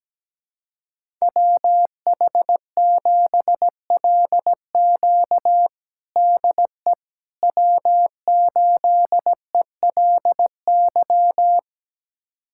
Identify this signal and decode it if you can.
Morse code, heard as a keyed tone, about 17 wpm, 700 Hz. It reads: WH7LQ DE W8ELY
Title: USA Callsign DE Callsign - 5 Characters - 17wpm